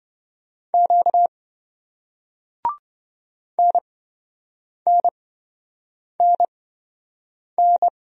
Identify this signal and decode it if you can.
Morse code, heard as a keyed tone, about 27 wpm, spaced as for 7 wpm, 700 Hz. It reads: Q NNNN